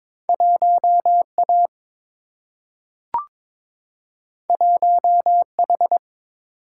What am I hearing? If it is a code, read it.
Morse code, heard as a keyed tone, about 22 wpm, 700 Hz. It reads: 1A 1H
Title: Single-Letter Single-Number 22wpm